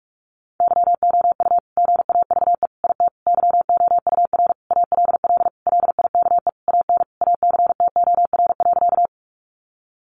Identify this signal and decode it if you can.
Morse code, heard as a keyed tone, about 44 wpm, 700 Hz. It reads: YOU GAVE IT YOUR ALL LIKE AN ACTOR.
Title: Sentences from Top 1500 Words 45wpm